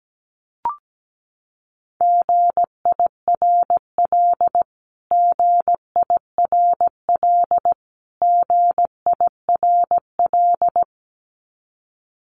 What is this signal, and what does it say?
Morse code, heard as a keyed tone, about 17 wpm, 700 Hz. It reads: GIRL GIRL GIRL
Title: Top 500 Words Repeated 17wpm